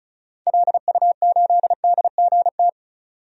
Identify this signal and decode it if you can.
Morse code, heard as a keyed tone, about 34 wpm, 700 Hz. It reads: LU8DGT
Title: DXCC-100 Call Signs - 6 Characters - 35wpm